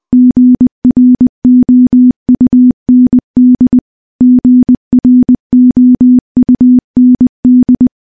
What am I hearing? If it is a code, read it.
Morse code, heard as a keyed tone, about 20 wpm, 250 Hz. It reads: GROUND GROUND
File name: morse-training-20WPM